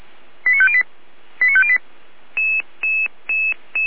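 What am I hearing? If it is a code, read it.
Five-tone selective call (ZVEI), heard as quick runs of five tones, tones of about 70 ms. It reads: 89508 89508
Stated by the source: ZVEI five-tone selective call 89320 received on SDR.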